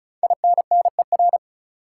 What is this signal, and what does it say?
Morse code, heard as a keyed tone, about 36 wpm, 700 Hz. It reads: INNER
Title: Five Letter Words - 35wpm